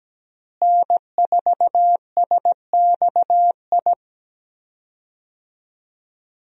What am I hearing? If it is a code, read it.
Morse code, heard as a keyed tone, about 17 wpm, 700 Hz. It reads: N4SXI
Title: USA Call Signs - 5 Characters - Repeated 3x - 17wpm